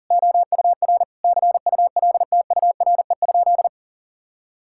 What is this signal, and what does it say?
Morse code, heard as a keyed tone, about 40 wpm, 690 Hz. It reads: OUR CULTURE?
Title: Sentences from Top 1000 Words - Review Part 6 - 40wpm